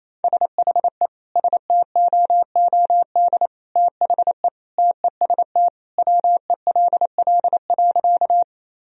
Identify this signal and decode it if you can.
Morse code, heard as a keyed tone, about 28 wpm, 700 Hz. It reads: SHE STOOD THE TEST WELL.
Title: Sentences from Top 1000 Words - Review Part 3 - 28wpm